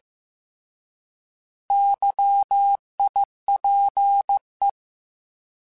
Morse code, heard as a keyed tone, about 15 wpm, 800 Hz. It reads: YIPE
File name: Alphabetical / Alphabetical mp3s / Ys @13